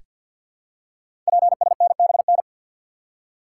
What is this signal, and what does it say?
Morse code, heard as a keyed tone, about 46 wpm, 700 Hz. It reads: PSNBN
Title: Code Groups - 5 Letters - 50wpm